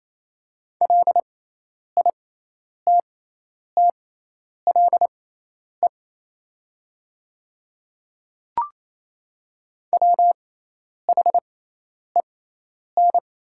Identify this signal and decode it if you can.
Morse code, heard as a keyed tone, about 27 wpm, spaced as for 10 wpm, 700 Hz. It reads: LITTLE WHEN
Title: 6x Character Spacing - Top 200 Words - 28wpm